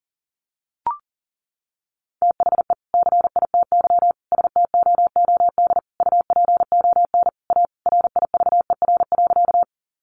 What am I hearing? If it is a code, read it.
Morse code, heard as a keyed tone, about 40 wpm, 700 Hz. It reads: THE CITY STOOD UPON A RIVER.
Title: Sentences from Top 1100 Words 40wpm